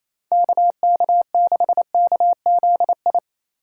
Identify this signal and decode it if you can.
Morse code, heard as a keyed tone, about 28 wpm, 700 Hz. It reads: KK6KZI
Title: USA Call Signs - 6 Characters - 2x Character Spacing - 28wpm